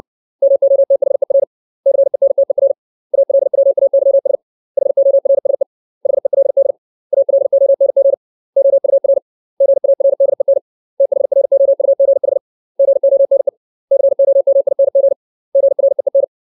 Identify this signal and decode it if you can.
Morse code, heard as a keyed tone, about 60 wpm, 550 Hz. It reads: MOTHER CENTER ALWAYS HORSE HELD ALONG OUR GARDEN THROUGH GONE COMING GREEN
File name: 632 words / 632 Words 60WPM